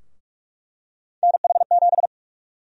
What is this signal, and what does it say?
Morse code, heard as a keyed tone, about 48 wpm, 700 Hz. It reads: NH7
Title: USA Call Sign Prefixes - 45wpm